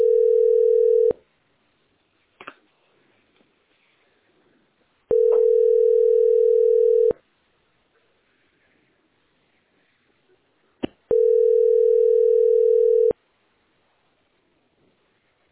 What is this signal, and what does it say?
Ringback tone (North America), 440 + 480 Hz.
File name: ac718eb7-8e64-4026-854e